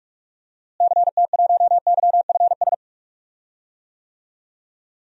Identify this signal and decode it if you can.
Morse code, heard as a keyed tone, about 46 wpm, 700 Hz. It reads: CT1YFS
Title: DXCC-100 Call Signs - 6 Characters - Repeated 3x - 45wpm